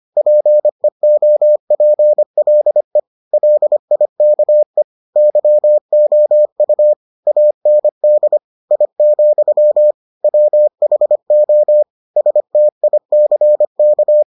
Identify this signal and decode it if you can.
Morse code, heard as a keyed tone, about 25 wpm, 600 Hz. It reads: PEOPLE LIKE YOU AND I, WHO STICK